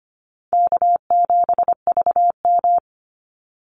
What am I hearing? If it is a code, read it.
Morse code, heard as a keyed tone, about 25 wpm, 700 Hz. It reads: K7VM
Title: Easy Call Signs Repeated 2x - 25wpm